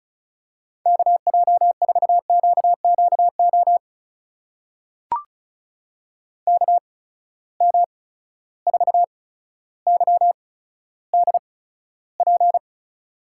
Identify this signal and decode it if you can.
Morse code, heard as a keyed tone, about 36 wpm, spaced as for 15 wpm, 700 Hz. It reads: KJ4QQO KM4YDP
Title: USA Call Signs - 6 Characters - 8x Character Spacing - 35wpm